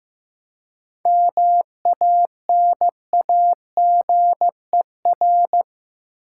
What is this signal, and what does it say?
Morse code, heard as a keyed tone, about 15 wpm, 700 Hz. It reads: MANAGER
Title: Seven Letter Words - 15wpm